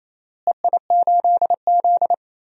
Morse code, heard as a keyed tone, about 28 wpm, 700 Hz. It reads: EI8Z
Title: DXCC-100 Call Signs - 4 Characters - Repeated 3x - 28wpm